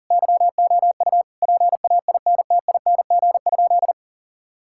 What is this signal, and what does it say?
Morse code, heard as a keyed tone, about 37 wpm, 700 Hz. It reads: YOU PAINTING?
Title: Sentences from Top 1000 Words - Review Part 6 - 40wpm